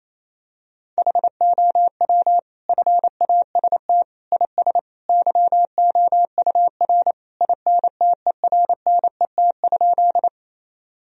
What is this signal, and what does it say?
Morse code, heard as a keyed tone, about 28 wpm, 700 Hz. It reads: HOW FAST IS YOUR INTERNET?
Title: Sentences from Top 1000 Words - Review Part 4 - 28wpm